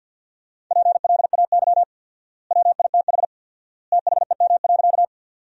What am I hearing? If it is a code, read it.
Morse code, heard as a keyed tone, about 50 wpm, 700 Hz. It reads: PLAY WITH THEM.
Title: 4x Word-Spacing - Sentences from Top 200 Words - 50wpm